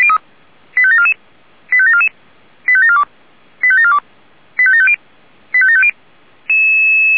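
Five-tone selective call (ZVEI), heard as quick runs of five tones, tones of about 70 ms. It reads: 85744 85744 85742 85742 85750 85750
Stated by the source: Five-tone ZVEI selective call received by SDR, archived by the fire brigade Feuerwehr Orlamünde.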